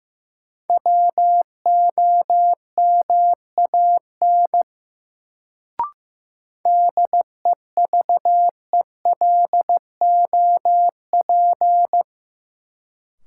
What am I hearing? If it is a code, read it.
Morse code, heard as a keyed tone, about 15 wpm, 700 Hz. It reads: WOMAN DEVELOP
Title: Sentences from Top 1000 Words - Review Part 2 - New Word Drill - 15wpm